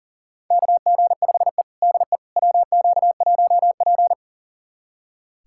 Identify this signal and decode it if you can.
Morse code, heard as a keyed tone, about 40 wpm, 700 Hz. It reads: KG5E DE WQ1P
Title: USA Callsign DE Callsign - 4 Characters - 40wpm